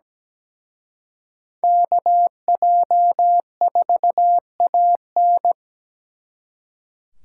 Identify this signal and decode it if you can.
Morse code, heard as a keyed tone, about 20 wpm, 700 Hz. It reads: KJ4AN
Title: Speed Racing - USA 2x2 Call Signs 17wpm